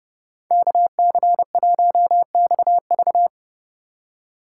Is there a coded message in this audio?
Yes: Morse code from 0.5 to 3.3 s, heard as a keyed tone, about 38 wpm, 700 Hz. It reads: KC1XV